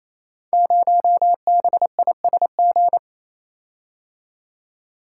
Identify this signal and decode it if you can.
Morse code, heard as a keyed tone, about 27 wpm, 700 Hz. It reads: 0BISG
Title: Code Groups - 5 Letter:Numbers - 28wpm